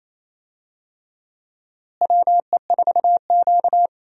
Morse code, heard as a keyed tone, about 28 wpm, 700 Hz. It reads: WE4Q
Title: USA Call Signs - 4 Characters - Repeated 3x - 28wpm